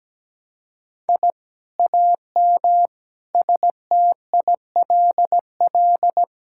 Morse code, heard as a keyed tone, about 17 wpm, 700 Hz. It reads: I AM STILL
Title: Sentences from Top 1000 Words - Review Part 6 - 17wpm